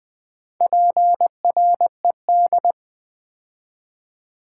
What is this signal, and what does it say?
Morse code, heard as a keyed tone, about 20 wpm, 700 Hz. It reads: PRED
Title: Build A Word - 4 Letters At A Time - 11-Letter Words - 20wpm